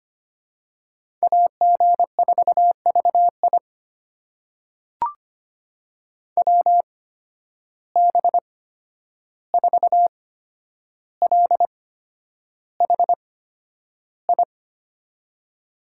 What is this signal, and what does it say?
Morse code, heard as a keyed tone, about 25 wpm, spaced as for 10 wpm, 700 Hz. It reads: AG4VI WB4LHI
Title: USA Call Signs - 8x Character Spacing - 25wpm